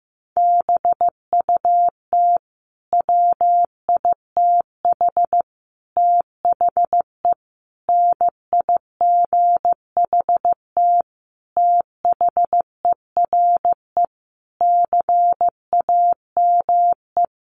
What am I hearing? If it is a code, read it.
Morse code, heard as a keyed tone, about 14 wpm, 700 Hz. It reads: BUT WITH THE NIGHT THERE CAME
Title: Sentences from Top 1500 Words 15wpm